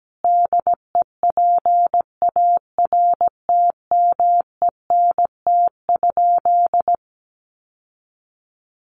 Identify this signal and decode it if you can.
Morse code, heard as a keyed tone, about 17 wpm, 700 Hz. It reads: DEPARTMENT?
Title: Sentences from Top 1200 Words 17wpm